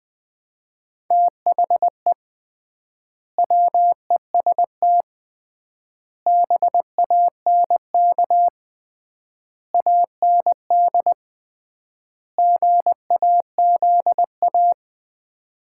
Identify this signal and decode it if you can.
Morse code, heard as a keyed tone, about 21 wpm, 700 Hz. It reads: THE WEST BANK AND GAZA
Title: 3x Word-Spacing - Sets of 5 Words - 20wpm